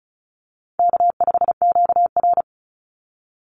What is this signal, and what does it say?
Morse code, heard as a keyed tone, about 34 wpm, 700 Hz. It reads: K5QR
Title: NA SOTA Chaser Callsigns 35wpm